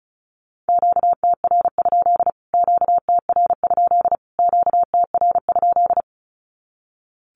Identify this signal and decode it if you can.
Morse code, heard as a keyed tone, about 34 wpm, 700 Hz. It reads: QTR? QTR? QTR?
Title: Q-Codes Repeated 3x 35wpm